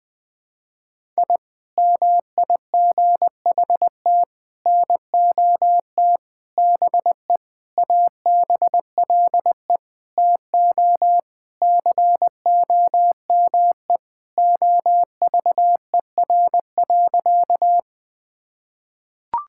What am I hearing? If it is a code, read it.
Morse code, heard as a keyed tone, about 21 wpm, 700 Hz. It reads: I MIGHT NOT BE ABLE TO COME OVER.
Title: Sentences with 4-Word Phrases - Spoken Beginning - 20wpm